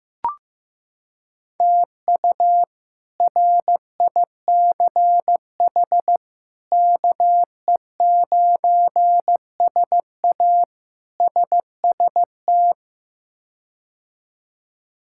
Morse code, heard as a keyed tone, about 16 wpm, 700 Hz. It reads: TU RICH KE9SA SST
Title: K1USN SST QSO Practice - 15wpm